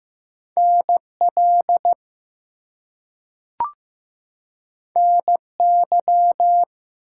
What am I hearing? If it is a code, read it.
Morse code, heard as a keyed tone, about 15 wpm, 700 Hz. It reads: NL NY
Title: US States and Canadian Provinces 15wpm